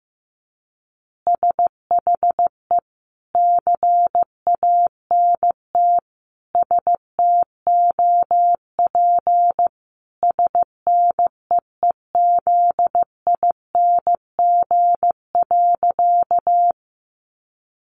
Morse code, heard as a keyed tone, about 15 wpm, 700 Hz. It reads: SHE CANT STOP SNEEZING.
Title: Easy Phrases 15wpm